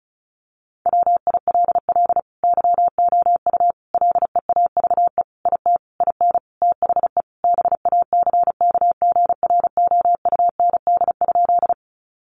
Morse code, heard as a keyed tone, about 34 wpm, 700 Hz. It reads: WILL YOU LEAVE IT IN THE BACKGROUND?
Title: Sentences from Top 1300 Words 35wpm